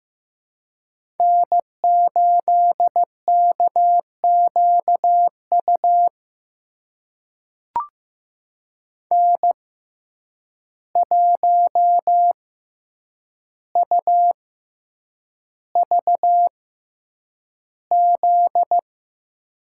Morse code, heard as a keyed tone, about 15 wpm, spaced as for 8 wpm, 700 Hz. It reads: N8KQU N1UVZ
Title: USA Call Signs - 5 Characters - 6x Character Spacing - 15wpm